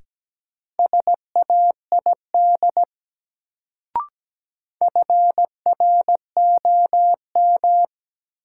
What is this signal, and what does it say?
Morse code, heard as a keyed tone, about 17 wpm, 700 Hz. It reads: SAID FROM
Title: Sets of 2 Words - New Word Drill - 17wpm